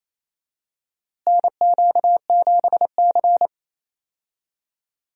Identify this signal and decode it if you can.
Morse code, heard as a keyed tone, about 27 wpm, 700 Hz. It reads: NQ7C
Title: USA Call Signs - 4 Characters - Repeated 3x - 28wpm